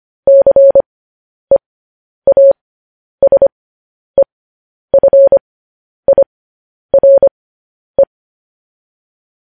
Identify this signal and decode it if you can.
Morse code, heard as a keyed tone, about 25 wpm, spaced as for 10 wpm, 550 Hz. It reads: CEASEFIRE
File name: podcast / News headlines-1x-25x10